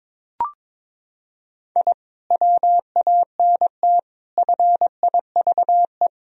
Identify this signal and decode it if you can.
Morse code, heard as a keyed tone, about 22 wpm, 700 Hz. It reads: I WANT FIVE
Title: Sentences from Top 1000 Words - Review Part 2 - 22wpm